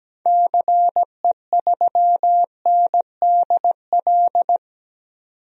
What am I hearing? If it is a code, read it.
Morse code, heard as a keyed tone, about 17 wpm, 700 Hz. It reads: CE3NDL
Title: DXCC-100 Call Signs - 6 Characters - 17wpm